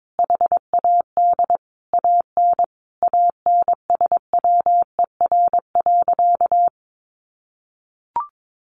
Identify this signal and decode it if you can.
Morse code, heard as a keyed tone, about 22 wpm, 700 Hz. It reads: HAD AN ANSWER.
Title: Sentences from Top 1200 Words 22wpm